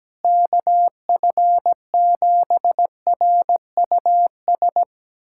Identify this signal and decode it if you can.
Morse code, heard as a keyed tone, about 17 wpm, 700 Hz. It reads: KF7RUS
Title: USA Callsign DE Callsign - 6 Characters - 17wpm